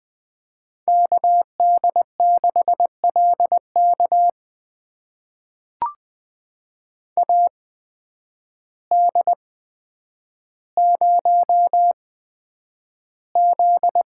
Morse code, heard as a keyed tone, about 20 wpm, spaced as for 9 wpm, 700 Hz. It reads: KD6LK AD0Z
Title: USA Call Signs - 8x Character Spacing - 20wpm